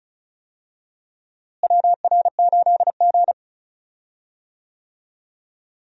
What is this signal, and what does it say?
Morse code, heard as a keyed tone, about 36 wpm, 700 Hz. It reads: WR8G
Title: USA Call Signs - 4 Characters - Repeated 2x - 35wpm